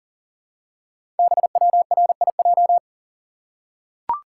Morse code, heard as a keyed tone, about 40 wpm, 700 Hz. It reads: BWRIJ
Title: Code Groups - 5 Letters - 40wpm